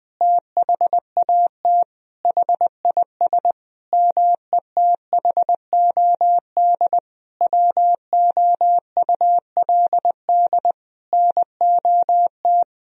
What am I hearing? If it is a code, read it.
Morse code, heard as a keyed tone, about 21 wpm, 700 Hz. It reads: THAT HIS METHOD WOULD NOT
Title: Sentences from Top 1000 Words - Review Part 3 - 20wpm